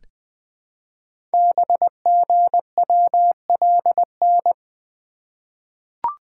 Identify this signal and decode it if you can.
Morse code, heard as a keyed tone, about 20 wpm, 700 Hz. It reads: BGWLN
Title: Code Groups - 5 Letters - 20wpm